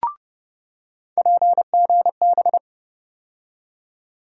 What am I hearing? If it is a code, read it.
Morse code, heard as a keyed tone, about 29 wpm, 700 Hz. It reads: PGB
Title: USA Call Sign Postfixes - 3 Letters 30wpm